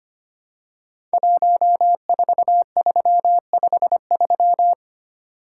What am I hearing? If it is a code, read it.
Morse code, heard as a keyed tone, about 25 wpm, 700 Hz. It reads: 14353
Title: Code Groups - 5 Letter:Numbers - 25wpm